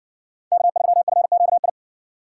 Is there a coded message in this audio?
Yes: Morse code from 0.5 to 1.7 s, heard as a keyed tone, about 60 wpm, 700 Hz. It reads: K3FQI